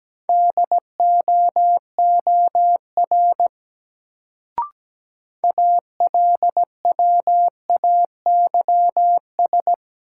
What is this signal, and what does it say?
Morse code, heard as a keyed tone, about 17 wpm, 700 Hz. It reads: DOOR ALWAYS
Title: Sentences from Top 300 Words - New Word Drill - 17wpm